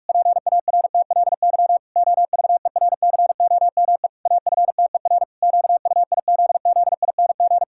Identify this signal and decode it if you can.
Morse code, heard as a keyed tone, about 46 wpm, 700 Hz. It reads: PARTLY OVERCOME AFTER QUIZZING
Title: Pangram Challenge v2 - 45wpm